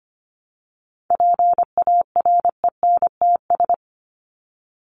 Morse code, heard as a keyed tone, about 25 wpm, 700 Hz. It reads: PARENTS
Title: ES:S Suffix - Top 100 Words with Base Words 25wpm